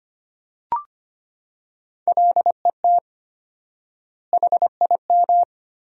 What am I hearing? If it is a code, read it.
Morse code, heard as a keyed tone, about 25 wpm, 700 Hz. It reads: LET HIM
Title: 4x Word-Spacing - Sentences from Top 300 Words - 25wpm